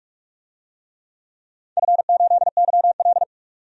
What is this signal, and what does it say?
Morse code, heard as a keyed tone, about 46 wpm, 700 Hz. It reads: F8YL
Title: DXCC-100 Call Signs - 4 Characters - 45wpm